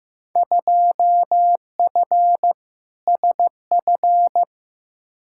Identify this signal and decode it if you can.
Morse code, heard as a keyed tone, about 15 wpm, 700 Hz. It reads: 2F SF
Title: Field Day Exchanges - 15wpm